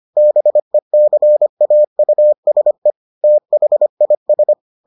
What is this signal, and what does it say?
Morse code, heard as a keyed tone, about 25 wpm, 600 Hz. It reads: BECAUSE THIS